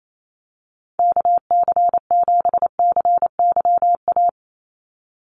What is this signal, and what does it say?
Morse code, heard as a keyed tone, about 28 wpm, 700 Hz. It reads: KC7CYA
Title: USA Call Signs 28wpm